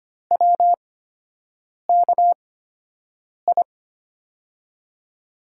Morse code, heard as a keyed tone, about 25 wpm, spaced as for 7 wpm, 700 Hz. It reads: WKI
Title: USA Call Signs - 6 Characters - 8x Character Spacing - 25wpm